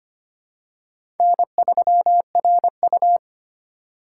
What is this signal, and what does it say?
Morse code, heard as a keyed tone, about 25 wpm, 700 Hz. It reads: N3RU
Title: USA Call Signs - 4 Characters - 3x Character Spacing - 25wpm